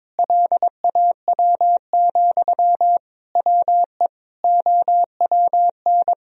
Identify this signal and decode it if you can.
Morse code, heard as a keyed tone, about 22 wpm, 700 Hz. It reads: LAW, WE OWN
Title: Sentences from Top 1000 Words - Review Part 4 - 22wpm